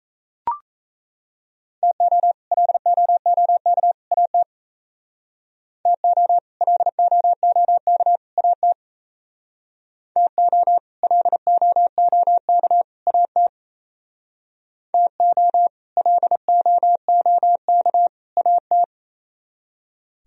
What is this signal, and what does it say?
Morse code, heard as a keyed tone, about 34 wpm, 700 Hz. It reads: TO LOOK AT TO LOOK AT TO LOOK AT TO LOOK AT
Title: Speed-Racing - Sets of 3 Words 28wpm